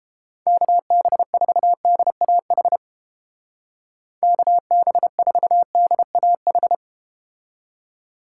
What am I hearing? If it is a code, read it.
Morse code, heard as a keyed tone, about 30 wpm, 700 Hz. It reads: KB4DAH KB4DAH